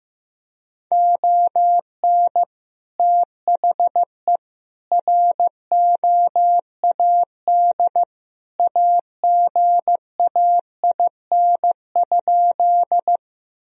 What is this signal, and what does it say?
Morse code, heard as a keyed tone, about 15 wpm, 700 Hz. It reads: ON THE ROAD AGAIN?
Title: Sentences from Top 1000 Words - Review Part 2 - 15wpm